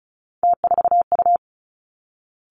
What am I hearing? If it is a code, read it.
Morse code, heard as a keyed tone, about 36 wpm, 700 Hz. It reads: T4U
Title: Three Challenging Letter-Number Combinations 35wpm